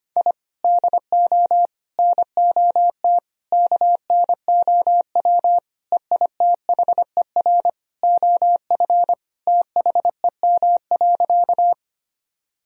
Morse code, heard as a keyed tone, about 25 wpm, 700 Hz. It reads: I DO NOT KNOW EITHER OF THEM.
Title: Sentences from Top 1000 Words - Review Part 4 - 25wpm